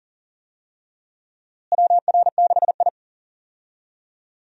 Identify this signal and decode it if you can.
Morse code, heard as a keyed tone, about 40 wpm, 700 Hz. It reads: WR6I
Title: USA Call Signs - 4 Characters - Repeated 2x - 40wpm